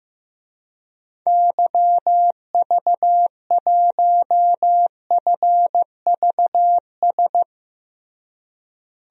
Morse code, heard as a keyed tone, about 15 wpm, 700 Hz. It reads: YV1FVS
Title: DXCC-100 Call Signs - 6 Characters - Repeated 2x - 15wpm